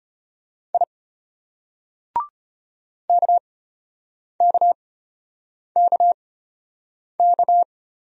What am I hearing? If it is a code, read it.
Morse code, heard as a keyed tone, about 32 wpm, spaced as for 8 wpm, 700 Hz. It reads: I KKKK